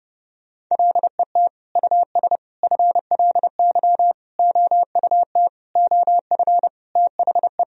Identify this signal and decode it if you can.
Morse code, heard as a keyed tone, about 30 wpm, 700 Hz. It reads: LET US FLY OUT OF THE
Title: Sentences from Top 1000 Words - Review Part 2 - 30wpm